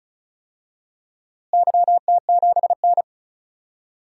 Morse code, heard as a keyed tone, about 36 wpm, 700 Hz. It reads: YT7N
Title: DXCC-100 Call Signs - 4 Characters - Repeated 3x - 35wpm